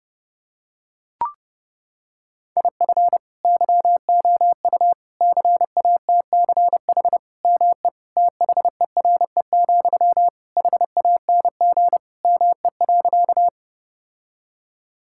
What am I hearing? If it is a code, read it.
Morse code, heard as a keyed tone, about 30 wpm, 700 Hz. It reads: IF YOU CATCH ME THERE, HANG ME.
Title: Sentences from Top 1600 Words 30wpm